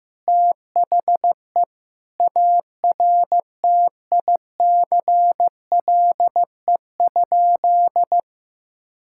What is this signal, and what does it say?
Morse code, heard as a keyed tone, about 15 wpm, 700 Hz. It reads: THE ARTICLE?
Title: Sentences from Top 1000 Words - Review Part 5 - 15wpm